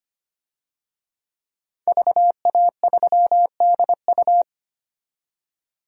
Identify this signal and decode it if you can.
Morse code, heard as a keyed tone, about 25 wpm, 700 Hz. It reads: VA3DU
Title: DXCC-100 Call Signs - 5 Characters - 25wpm